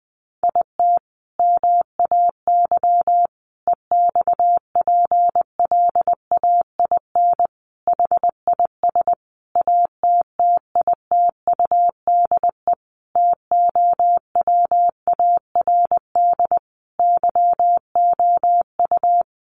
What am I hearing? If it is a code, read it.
Morse code, heard as a keyed tone, about 20 wpm, 700 Hz. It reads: IT MAY EXPLAIN HIS ATTITUDE TOWARD YOU
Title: Sentences from Top 1300 Words 20wpm